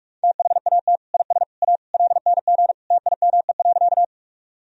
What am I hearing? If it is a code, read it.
Morse code, heard as a keyed tone, about 41 wpm, 700 Hz. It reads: THAT IS A LNG TIME.
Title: Sentences from Top 100 Words and Abbreviations 45wpm